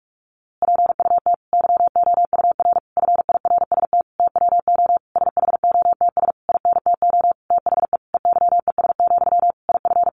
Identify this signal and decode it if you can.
Morse code, heard as a keyed tone, about 44 wpm, 700 Hz. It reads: PUT YOUR FIRST TWO SHOTS INTO THE EYES, IF
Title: A Sound Of Thunder 45wpm